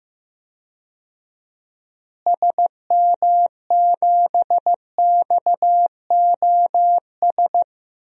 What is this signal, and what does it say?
Morse code, heard as a keyed tone, about 15 wpm, 700 Hz. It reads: SM7XOS
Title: DXCC-100 Call Signs - 6 Characters - Repeated 2x - 15wpm